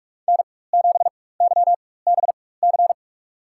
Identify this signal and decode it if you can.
Morse code, heard as a keyed tone, about 46 wpm, spaced as for 21 wpm, 700 Hz. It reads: N7YBC
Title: USA Call Signs - 5 Characters - 4x Character Spacing - 45wpm